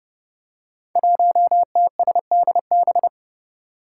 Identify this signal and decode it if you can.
Morse code, heard as a keyed tone, about 30 wpm, 700 Hz. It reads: 1TSDB
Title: Five Challenging Letter-Number Combinations 30wpm